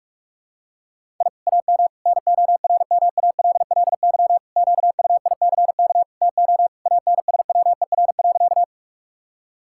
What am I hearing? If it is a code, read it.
Morse code, heard as a keyed tone, about 46 wpm, 700 Hz. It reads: I AM NORMALLY QUICK TO ANSWER.
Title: Sentences from Top 1000 Words - Review Part 6 - 45wpm